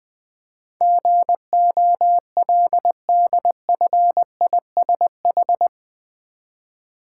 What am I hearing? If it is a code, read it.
Morse code, heard as a keyed tone, about 20 wpm, 700 Hz. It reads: GOLDFISH
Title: Easy Words with Hints - 20wpm